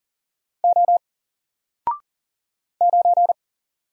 Morse code, heard as a keyed tone, about 40 wpm, 700 Hz. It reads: O 9
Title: Single Letter-Number 40wpm